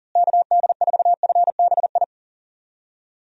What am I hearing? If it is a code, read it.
Morse code, heard as a keyed tone, about 40 wpm, 700 Hz. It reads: KD4FBI
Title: USA Call Signs - 6 Characters - Repeated 3x - 40wpm